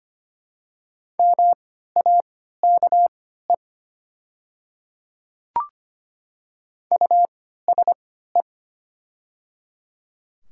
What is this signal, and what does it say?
Morse code, heard as a keyed tone, about 25 wpm, spaced as for 14 wpm, 700 Hz. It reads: MAKE USE
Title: 3x Character Spacing - Top 200 Words - 25wpm